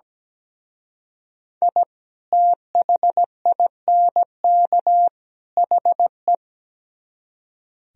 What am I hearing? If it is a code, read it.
Morse code, heard as a keyed tone, about 20 wpm, 700 Hz. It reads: I THINK HE
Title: Speed-Racing - Sets of 3 Words 17wpm